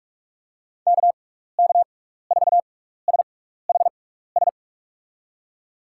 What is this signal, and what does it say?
Morse code, heard as a keyed tone, about 46 wpm, spaced as for 15 wpm, 700 Hz. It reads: KK4SHS
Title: USA Call Signs - 6 Characters - 6x Character Spacing - 45wpm